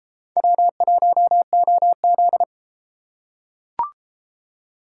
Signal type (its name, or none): Morse code